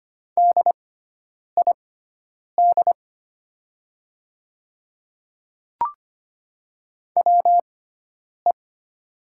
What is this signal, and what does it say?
Morse code, heard as a keyed tone, about 25 wpm, spaced as for 9 wpm, 700 Hz. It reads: DID WE
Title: 6x Character Spacing - Top 100 Words - 25wpm